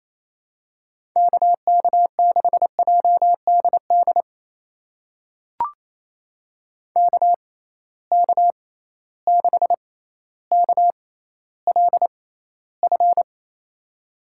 Morse code, heard as a keyed tone, about 28 wpm, spaced as for 14 wpm, 700 Hz. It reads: KK6JDD KK6KLF
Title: USA Call Signs - 6 Characters - 6x Character Spacing - 28wpm